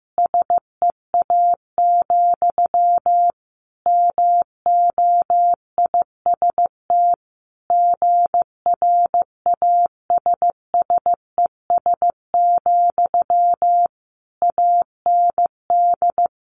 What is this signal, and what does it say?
Morse code, heard as a keyed tone, about 15 wpm, 700 Hz. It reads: SEA, MOIST GRASSES, AND
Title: A Sound Of Thunder 15wpm